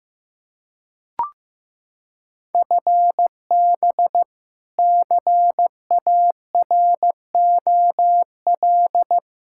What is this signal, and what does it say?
Morse code, heard as a keyed tone, about 16 wpm, 700 Hz. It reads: FB CAROL
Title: Practice QSO - Exchange 3 - More Info - 15wpm